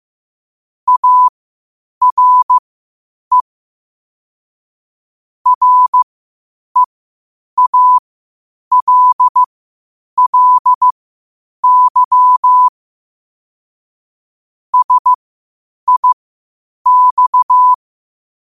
Morse code, heard as a keyed tone, about 15 wpm, spaced as for 8 wpm, 1000 Hz. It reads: ARE REALLY SIX
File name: qotdfiles / 8wpm